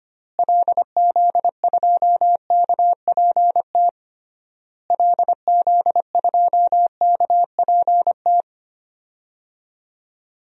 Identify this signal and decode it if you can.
Morse code, heard as a keyed tone, about 25 wpm, 700 Hz. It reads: LZ2KPT LZ2KPT
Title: DXCC-100 Call Signs - 6 Characters - Repeated 3x - 25wpm